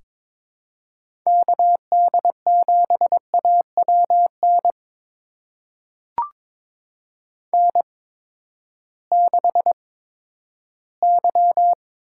Morse code, heard as a keyed tone, about 22 wpm, spaced as for 13 wpm, 700 Hz. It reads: KD7AWN N6Y
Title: USA Call Signs - 8x Character Spacing - 22wpm